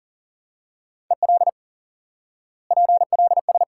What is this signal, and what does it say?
Morse code, heard as a keyed tone, about 40 wpm, 700 Hz. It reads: EL PLS
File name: IQE Course 3 - Lesson 89 - YAGI - Two Elements - 40wpm